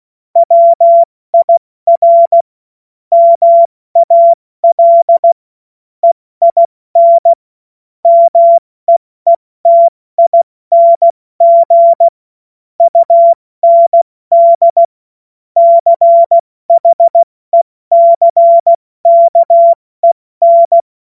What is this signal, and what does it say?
Morse code, heard as a keyed tone, about 16 wpm, spaced as for 14 wpm, 680 Hz. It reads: WIR MAL EIN MEETING UND CHECKEN
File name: Morsekurs / Witze / Witz 14. Dezember 2021, 70 F80